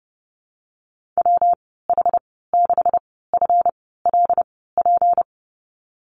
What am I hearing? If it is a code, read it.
Morse code, heard as a keyed tone, about 32 wpm, spaced as for 17 wpm, 700 Hz. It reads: WH6FLP
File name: Lesson 37g - Letter x - Call Signs 30wpm